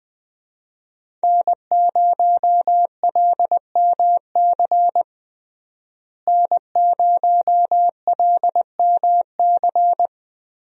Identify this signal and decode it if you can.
Morse code, heard as a keyed tone, about 20 wpm, 700 Hz. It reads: N0LMC N0LMC
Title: USA Call Signs - 5 Characters - Repeated 3x - 20wpm